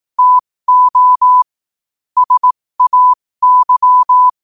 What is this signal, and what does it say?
Morse code, heard as a keyed tone, about 18 wpm, spaced as for 15 wpm, 1000 Hz. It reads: TO SAY
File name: qotdfiles / 15wpm